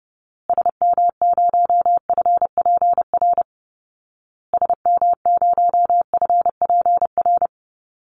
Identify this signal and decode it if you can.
Morse code, heard as a keyed tone, about 30 wpm, 700 Hz. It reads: SM0FPR SM0FPR
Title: Call Signs Repeated 2x - 30wpm